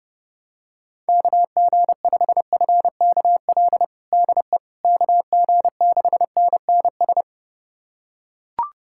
Morse code, heard as a keyed tone, about 30 wpm, 700 Hz. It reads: KG5FKL DE KG6NNS
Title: USA Callsign DE Callsign 30wpm